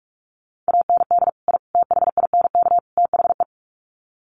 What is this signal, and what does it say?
Morse code, heard as a keyed tone, about 44 wpm, 700 Hz. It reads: AND I THINK THE
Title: Sets of 4 Words 45wpm